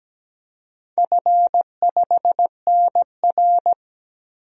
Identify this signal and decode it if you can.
Morse code, heard as a keyed tone, about 17 wpm, 700 Hz. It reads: F5NR
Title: DXCC-100 Call Signs - 4 Characters - 17wpm